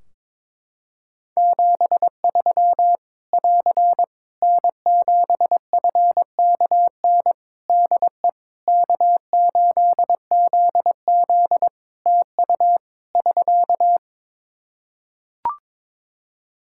Morse code, heard as a keyed tone, about 22 wpm, 700 Hz. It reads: 73 <AR> N7FKN DE K8ZZ TU <SK>
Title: Practice QSO - Exchange 4 - Ending - 22wpm